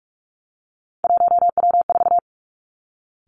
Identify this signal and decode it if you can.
Morse code, heard as a keyed tone, about 46 wpm, 700 Hz. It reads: 1W4
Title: Three Challenging Letter-Number Combinations 45wpm